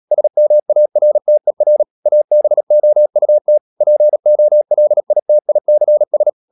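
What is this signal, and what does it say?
Morse code, heard as a keyed tone, about 37 wpm, 600 Hz. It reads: SMARTER ABOUT POLITICS
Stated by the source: Morse code practice at 35–39 wpm sending English text of Ken Follett's novel Never, from CW med Gnister.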